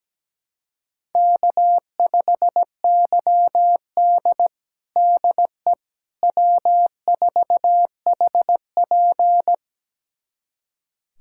Morse code, heard as a keyed tone, about 17 wpm, 700 Hz. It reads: K5YD DE W4HP
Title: USA Callsign DE Callsign - 4 Characters - 17wpm